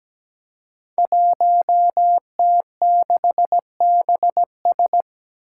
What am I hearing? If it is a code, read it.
Morse code, heard as a keyed tone, about 17 wpm, 700 Hz. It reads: 1T6BS
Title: Five Challenging Letter-Number Combinations 17wpm